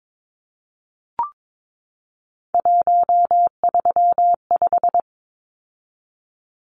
Morse code, heard as a keyed tone, about 22 wpm, 700 Hz. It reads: 135
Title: Numbers - 3-Digit 22wpm